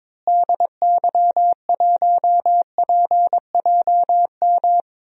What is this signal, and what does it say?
Morse code, heard as a keyed tone, about 22 wpm, 700 Hz. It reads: DY1PJM
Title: DXCC-100 Call Signs - 6 Characters - Repeated 3x - 22wpm